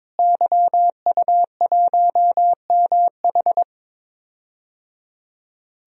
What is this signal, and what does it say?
Morse code, heard as a keyed tone, about 21 wpm, 700 Hz. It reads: YU1MH
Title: DXCC-100 Call Signs - 5 Characters - Repeated 3x - 22wpm